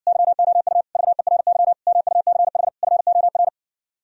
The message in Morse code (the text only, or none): YOU VERY MUCH FOR